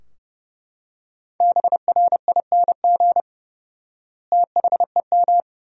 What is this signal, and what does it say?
Morse code, heard as a keyed tone, about 30 wpm, 700 Hz. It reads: BRING THEM
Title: 4x Word-Spacing - Sentences from Top 500 Words - 30wpm